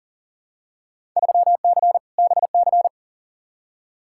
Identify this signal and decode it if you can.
Morse code, heard as a keyed tone, about 37 wpm, 700 Hz. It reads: 3C BC
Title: Field Day Exchanges Encore - 40wpm